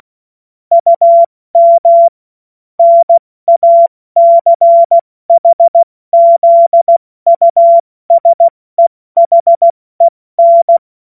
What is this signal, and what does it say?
Morse code, heard as a keyed tone, about 16 wpm, 680 Hz. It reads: UM NACHZUSEHEN
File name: Morsekurs / Witze / Witz 09 April 2024 70 F80